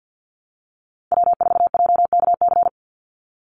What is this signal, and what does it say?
Morse code, heard as a keyed tone, about 50 wpm, 700 Hz. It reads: W4JKC
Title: NA SOTA Chaser Callsigns 50wpm